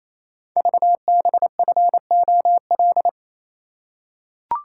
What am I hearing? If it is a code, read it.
Morse code, heard as a keyed tone, about 28 wpm, 700 Hz. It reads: VBFOL
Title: Code Groups - 5 Letters - 28wpm